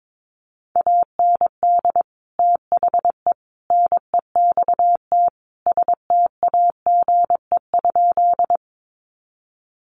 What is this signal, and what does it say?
Morse code, heard as a keyed tone, about 22 wpm, 700 Hz. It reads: AND THE NEXT STAGE?
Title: Sentences from Top 1200 Words 22wpm